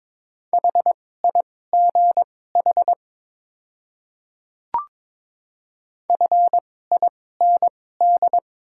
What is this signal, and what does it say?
Morse code, heard as a keyed tone, about 22 wpm, spaced as for 16 wpm, 700 Hz. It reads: HIGH FIND
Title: 2x Character Spacing - Top 200 Words - 22wpm